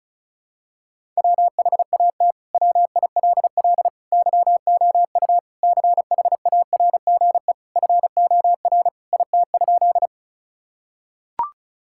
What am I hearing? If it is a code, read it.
Morse code, heard as a keyed tone, about 34 wpm, 700 Hz. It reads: WHAT WILL YOU CHARGE FOR IT?
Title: Sentences from Top 1000 Words - Review Part 6 - 35wpm